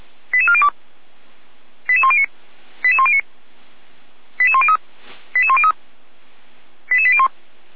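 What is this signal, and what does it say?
Five-tone selective call (ZVEI), heard as quick runs of five tones, tones of about 70 ms. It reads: 88402 88109 88109 88103 88103 88801
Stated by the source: ZVEI five-tone call received on an SDR, from the Feuerwehr Orlamuende (feuerwehr-orlamuende.org) audio archive.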